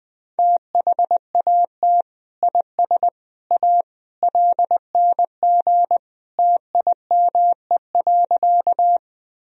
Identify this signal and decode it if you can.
Morse code, heard as a keyed tone, about 21 wpm, 700 Hz. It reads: THAT IS A LNG TIME.
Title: Sentences from Top 100 Words and Abbreviations 20wpm